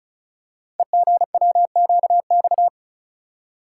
Morse code, heard as a keyed tone, about 33 wpm, 700 Hz. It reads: EGWQX
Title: Code Groups - 5 Letters - 35wpm